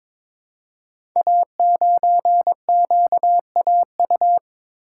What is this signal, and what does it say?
Morse code, heard as a keyed tone, about 22 wpm, 700 Hz. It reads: A9QAU
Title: Code Groups - 5 Letter:Numbers - 22wpm